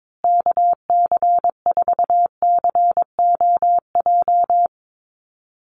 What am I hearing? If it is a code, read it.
Morse code, heard as a keyed tone, about 22 wpm, 700 Hz. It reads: KC4COJ
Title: USA Call Signs Repeated 2x 22wpm